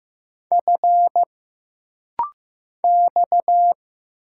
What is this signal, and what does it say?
Morse code, heard as a keyed tone, about 15 wpm, 700 Hz. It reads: F X
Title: Single Letters - Rapid-Fire 15wpm